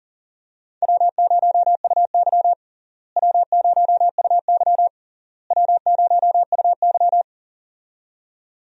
Morse code, heard as a keyed tone, about 37 wpm, 700 Hz. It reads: W0UY W0UY W0UY
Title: USA Call Signs - 4 Characters - Repeated 3x - 40wpm